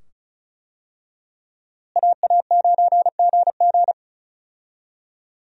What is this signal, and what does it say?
Morse code, heard as a keyed tone, about 36 wpm, 700 Hz. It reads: AA9GG
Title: USA Call Signs - 5 Characters - Repeated 2x - 35wpm